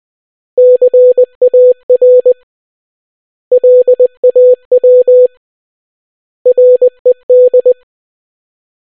Morse code, heard as a keyed tone, about 20 wpm, 500 Hz. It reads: CAR LAW RED